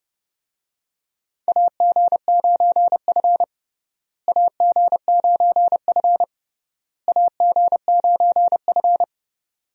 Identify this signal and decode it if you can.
Morse code, heard as a keyed tone, about 30 wpm, 700 Hz. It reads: AG9F AG9F AG9F
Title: USA Call Signs - 4 Characters - Repeated 3x - 30wpm